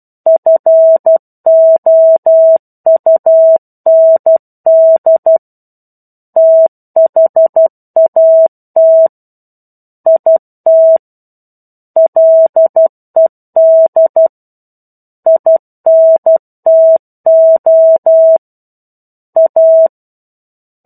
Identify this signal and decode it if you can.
Morse code, heard as a keyed tone, about 12 wpm, 650 Hz. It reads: FOUND THAT IT LED INTO A
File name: 01-Chapter-01 12 WPM